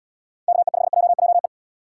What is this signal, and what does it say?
Morse code, heard as a keyed tone, about 69 wpm, 700 Hz. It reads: KE5JJE